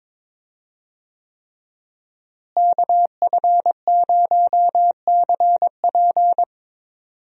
Morse code, heard as a keyed tone, about 22 wpm, 700 Hz. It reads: KF0CP
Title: USA Call Signs - 5 Characters - Repeated 3x - 22wpm